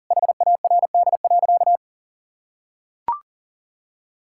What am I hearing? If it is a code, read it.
Morse code, heard as a keyed tone, about 40 wpm, 700 Hz. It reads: HARD.
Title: Pangram Challenge v2 - 40wpm